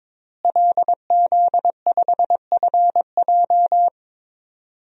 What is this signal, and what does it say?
Morse code, heard as a keyed tone, about 22 wpm, 700 Hz. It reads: LZ5FJ
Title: DXCC-100 Call Signs - 5 Characters - Repeated 2x - 22wpm